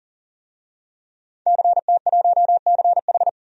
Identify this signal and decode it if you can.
Morse code, heard as a keyed tone, about 40 wpm, 700 Hz. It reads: CT1CH
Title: DXCC-100 Call Signs - 5 Characters - 40wpm